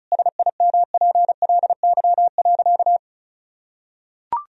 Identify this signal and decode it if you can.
Morse code, heard as a keyed tone, about 34 wpm, 700 Hz. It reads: SIMPLY.
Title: Sentences from Top 1000 Words - Review Part 4 - 35wpm